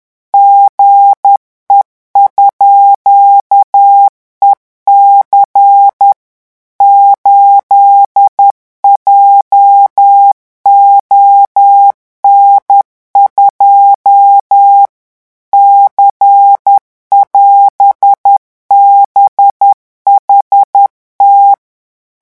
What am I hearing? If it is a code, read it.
Morse code, heard as a keyed tone, about 10 wpm, 800 Hz. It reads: GE_EC 8JON2 C<AS>BHT
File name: morse mp3 / revision17-vitesse 10 mots